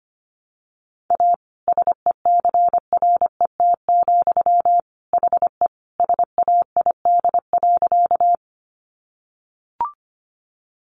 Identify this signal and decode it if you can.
Morse code, heard as a keyed tone, about 25 wpm, 700 Hz. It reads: A SECRET, HE SAID.
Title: Sentences from Top 1200 Words 25wpm